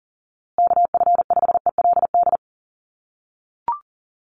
Morse code, heard as a keyed tone, about 40 wpm, 700 Hz. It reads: KF5ELD
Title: USA Call Signs Repeated 2x 40wpm